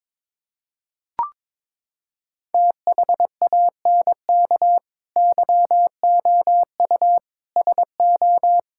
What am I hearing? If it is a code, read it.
Morse code, heard as a keyed tone, about 22 wpm, 700 Hz. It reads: THANK YOU SO
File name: practice / 22wpm / Sentences with 4-Word Phrases - 22wpm